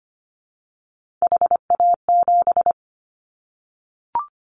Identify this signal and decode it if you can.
Morse code, heard as a keyed tone, about 25 wpm, 700 Hz. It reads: HA7
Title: Callsign Prefixes for Top 25 DXCC - Repeated 3x - 25wpm